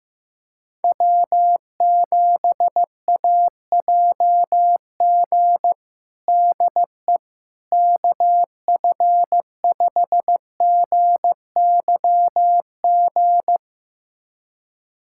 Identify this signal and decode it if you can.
Morse code, heard as a keyed tone, about 15 wpm, 700 Hz. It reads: W7AJG DE KF5GYG
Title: USA Callsign DE Callsign 15wpm